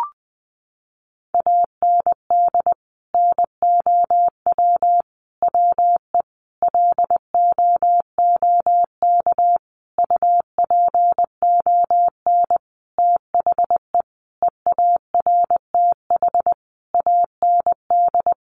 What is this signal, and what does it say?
Morse code, heard as a keyed tone, about 20 wpm, 700 Hz. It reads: AND NOW WE LOOK UPON THE EARTH AND
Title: Sentences from Top 1200 Words 20wpm